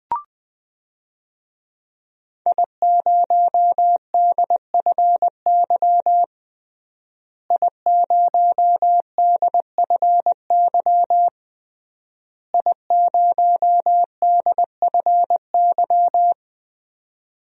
Morse code, heard as a keyed tone, about 20 wpm, 700 Hz. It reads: I0DFY I0DFY I0DFY
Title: DXCC-100 Call Signs - 5 Characters - Repeated 3x - 20wpm